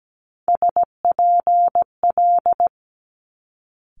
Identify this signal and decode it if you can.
Morse code, heard as a keyed tone, about 17 wpm, 700 Hz. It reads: SPL
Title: Multi-letter Phonemes Quick Fire 17wpm